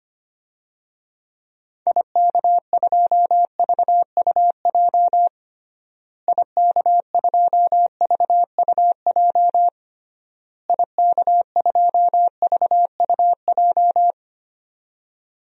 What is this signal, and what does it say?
Morse code, heard as a keyed tone, about 25 wpm, 700 Hz. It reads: IK2VUJ IK2VUJ IK2VUJ
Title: DXCC-100 Call Signs - 6 Characters - Repeated 3x - 25wpm